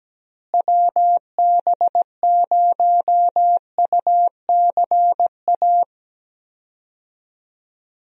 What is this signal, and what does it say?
Morse code, heard as a keyed tone, about 17 wpm, 700 Hz. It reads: WB0UCA
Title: USA Call Signs - 6 Characters - Repeated 2x - 17wpm